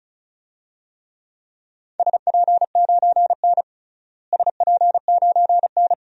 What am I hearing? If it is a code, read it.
Morse code, heard as a keyed tone, about 36 wpm, 700 Hz. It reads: SP9N SP9N
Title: DXCC-100 Call Signs - 4 Characters - Repeated 2x - 35wpm